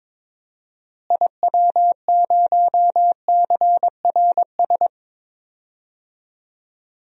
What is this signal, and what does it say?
Morse code, heard as a keyed tone, about 22 wpm, 700 Hz. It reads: IW0CRS
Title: DXCC-100 Call Signs - 6 Characters - Repeated 3x - 22wpm